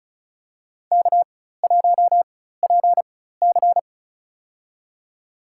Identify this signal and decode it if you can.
Morse code, heard as a keyed tone, about 36 wpm, spaced as for 16 wpm, 700 Hz. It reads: K1PC
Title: USA Call Signs - 4 Characters - 4x Character Spacing - 35wpm